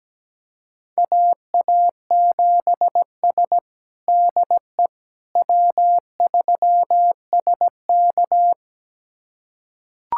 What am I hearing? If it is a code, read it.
Morse code, heard as a keyed tone, about 17 wpm, 700 Hz. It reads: AA7S DE W3SK
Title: USA Callsign DE Callsign - 4 Characters - 17wpm